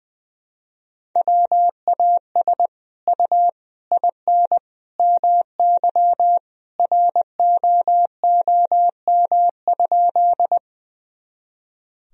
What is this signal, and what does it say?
Morse code, heard as a keyed tone, about 21 wpm, 700 Hz. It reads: WAS U IN MY ROOM?
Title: Sentences from Top 300 Words and Abbreviations 20wpm